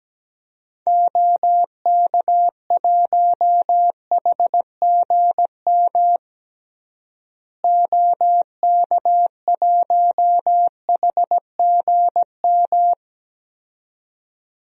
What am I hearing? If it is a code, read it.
Morse code, heard as a keyed tone, about 17 wpm, 700 Hz. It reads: OK1HGM OK1HGM
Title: DXCC-100 Call Signs - 6 Characters - Repeated 2x - 17wpm